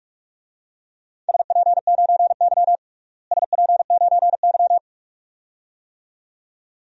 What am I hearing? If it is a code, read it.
Morse code, heard as a keyed tone, about 46 wpm, 700 Hz. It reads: SP9Y SP9Y
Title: DXCC-100 Call Signs - 4 Characters - Repeated 2x - 45wpm